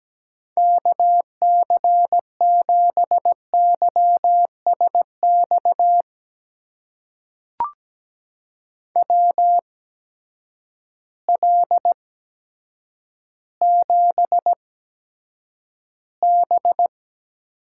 Morse code, heard as a keyed tone, about 17 wpm, spaced as for 8 wpm, 700 Hz. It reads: KC7YSX WL7B
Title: USA Call Signs - 6 Characters - 8x Character Spacing - 17wpm